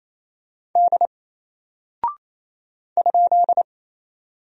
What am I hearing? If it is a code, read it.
Morse code, heard as a keyed tone, about 27 wpm, 700 Hz. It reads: D ?
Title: Single Character (Letter:Number:Punctuation:Prosign) Alt 28wpm